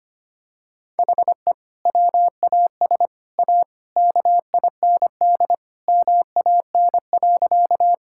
Morse code, heard as a keyed tone, about 25 wpm, 700 Hz. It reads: HE WAS A KIND MAN.
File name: practice / 25wpm / Sentences with 3-Word Phrases - 25wpm